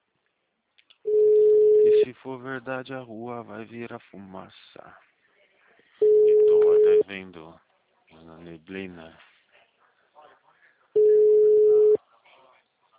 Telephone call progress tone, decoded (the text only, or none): ringback tone